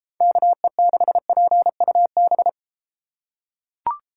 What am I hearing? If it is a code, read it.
Morse code, heard as a keyed tone, about 33 wpm, 700 Hz. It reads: KE6PUB